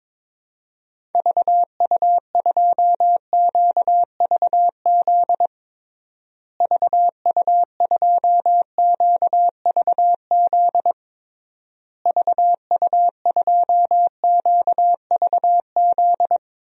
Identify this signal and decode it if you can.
Morse code, heard as a keyed tone, about 22 wpm, 700 Hz. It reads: VU2QVZ VU2QVZ VU2QVZ
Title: DXCC-100 Call Signs - 6 Characters - Repeated 3x - 22wpm